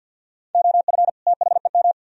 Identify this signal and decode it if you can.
Morse code, heard as a keyed tone, about 62 wpm, 700 Hz. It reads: OF THEM
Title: Speed-Racing - Sets of 2 Words 50wpm